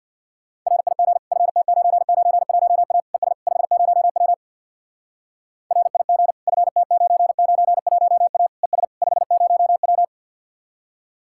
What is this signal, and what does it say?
Morse code, heard as a keyed tone, about 62 wpm, 700 Hz. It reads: RIG FT991A ES 50W RIG FT991A ES 50W
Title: Speed-Racing Common QSO Phrases - 50wpm